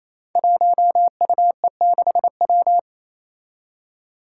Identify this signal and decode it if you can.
Morse code, heard as a keyed tone, about 28 wpm, 700 Hz. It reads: 1UE6W
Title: Five Challenging Letter-Number Combinations 28wpm